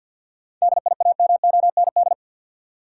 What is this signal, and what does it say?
Morse code, heard as a keyed tone, about 51 wpm, 700 Hz. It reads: DIAMOND
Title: Seven Letter Words - 50wpm